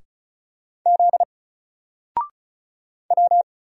Morse code, heard as a keyed tone, about 36 wpm, 700 Hz. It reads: Z W
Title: Single Character (Letter:Number:Punctuation:Prosign) Alt 35wpm